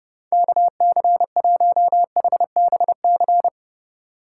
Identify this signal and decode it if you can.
Morse code, heard as a keyed tone, about 32 wpm, 700 Hz. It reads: KC1HBC